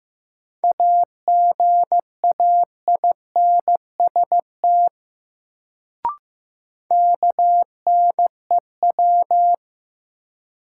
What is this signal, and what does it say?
Morse code, heard as a keyed tone, about 15 wpm, 700 Hz. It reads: AGAINST KNEW
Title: Sentences from Top 400 Words - New Word Drill - 15wpm